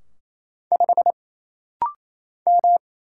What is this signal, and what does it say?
Morse code, heard as a keyed tone, about 27 wpm, 700 Hz. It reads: 5 M
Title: Single Letter-Number Alt - Rapid-Fire 28wpm